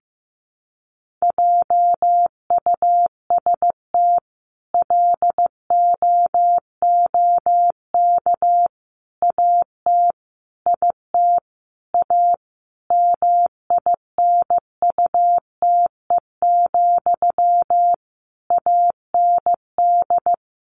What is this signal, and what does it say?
Morse code, heard as a keyed tone, about 15 wpm, 700 Hz. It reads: JUST LOOK AT IT A MINUTE, AND
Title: Sentences from Top 1000 Words 15wpm